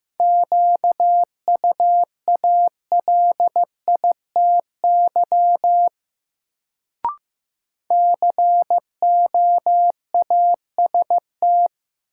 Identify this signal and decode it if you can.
Morse code, heard as a keyed tone, about 15 wpm, 700 Hz. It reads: QUALITY COAST
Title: Sentences from Top 1000 Words - Review Part 4 - New Word Drill - 15wpm